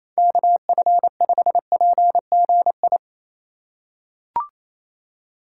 Morse code, heard as a keyed tone, about 28 wpm, 700 Hz. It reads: KF5PGI
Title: USA Call Signs - 3x Character Spacing - 28wpm